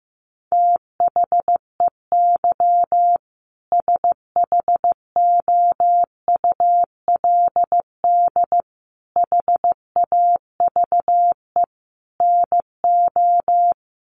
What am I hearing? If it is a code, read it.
Morse code, heard as a keyed tone, about 15 wpm, 700 Hz. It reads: THEY SHOULD HAVE NO
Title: Sentences from Top 1500 Words 15wpm